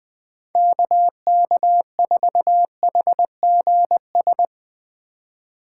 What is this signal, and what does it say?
Morse code, heard as a keyed tone, about 21 wpm, 700 Hz. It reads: KK4HGS
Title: Speed Racing - USA 2x3 Call Signs 20wpm